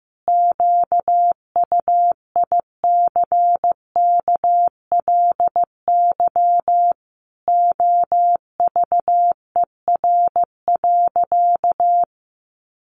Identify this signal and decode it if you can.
Morse code, heard as a keyed tone, about 15 wpm, 700 Hz. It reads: QUICKLY OVER.
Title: Sentences from Top 1100 Words 15wpm